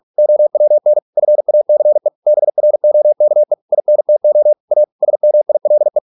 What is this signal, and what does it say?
Morse code, heard as a keyed tone, about 46 wpm, 600 Hz. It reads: OWN FACE BROKE INTO A SMILE